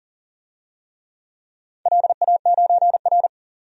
Morse code, heard as a keyed tone, about 41 wpm, 700 Hz. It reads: LA9R
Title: DXCC-100 Call Signs - 4 Characters - 40wpm